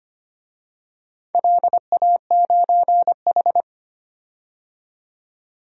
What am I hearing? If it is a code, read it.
Morse code, heard as a keyed tone, about 25 wpm, 700 Hz. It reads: LA9H
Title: DXCC-100 Call Signs - 4 Characters - 25wpm